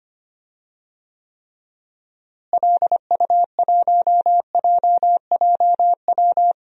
Morse code, heard as a keyed tone, about 25 wpm, 700 Hz. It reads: LU1JJW
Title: DXCC-100 Call Signs - 6 Characters - Repeated 3x - 25wpm